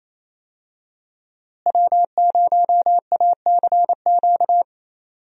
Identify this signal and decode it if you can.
Morse code, heard as a keyed tone, about 28 wpm, 700 Hz. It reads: W0ACQ
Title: USA Call Signs - 5 Characters - Repeated 3x - 28wpm